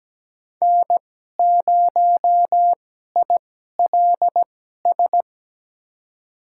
Morse code, heard as a keyed tone, about 17 wpm, spaced as for 12 wpm, 700 Hz. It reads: N0ILS
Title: USA Call Signs - 2x Character Spacing - 17wpm